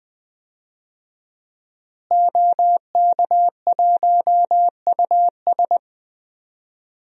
Morse code, heard as a keyed tone, about 20 wpm, 700 Hz. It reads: OK1US
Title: DXCC-100 Call Signs - 5 Characters - Repeated 3x - 20wpm